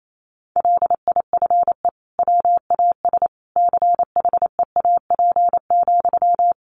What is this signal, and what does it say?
Morse code, heard as a keyed tone, about 28 wpm, 700 Hz. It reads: LIFE WAS CHEAP,
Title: Sentences from Top 1200 Words 28wpm